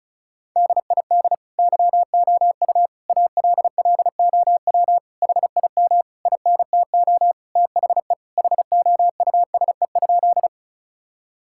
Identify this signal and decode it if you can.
Morse code, heard as a keyed tone, about 34 wpm, 700 Hz. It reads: DID YOU ALLOW HIM INTO THE HOUSE?
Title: Sentences from Top 1000 Words - Review Part 5 - 35wpm